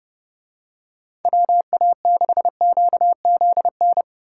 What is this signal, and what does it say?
Morse code, heard as a keyed tone, about 30 wpm, 700 Hz. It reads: WA6QZN